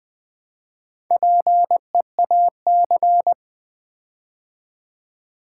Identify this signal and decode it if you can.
Morse code, heard as a keyed tone, about 20 wpm, 700 Hz. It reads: PEAC
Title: Build A Word - 4 Letters At A Time - 8-Letter Words - 20wpm